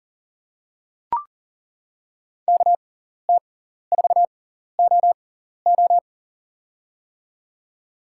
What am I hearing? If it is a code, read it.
Morse code, heard as a keyed tone, about 40 wpm, spaced as for 14 wpm, 700 Hz. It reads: KT4OO
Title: USA Call Signs - 5 Characters - 6x Character Spacing - 40wpm